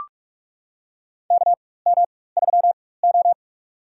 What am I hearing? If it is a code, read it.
Morse code, heard as a keyed tone, about 44 wpm, spaced as for 21 wpm, 700 Hz. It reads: KM3O
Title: USA Call Signs - 4 Characters - 4x Character Spacing - 45wpm